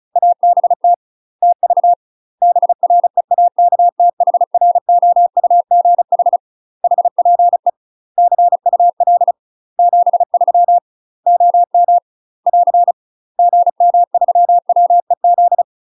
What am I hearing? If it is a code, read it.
Morse code, heard as a keyed tone, about 34 wpm, 700 Hz. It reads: ABT TV BREAKTHROUGH HPE CUL 73 OM <AR> GM3WEZ
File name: UK MASTER QSOs TRAINING / 35WPM / UK MASTER QSOs 0041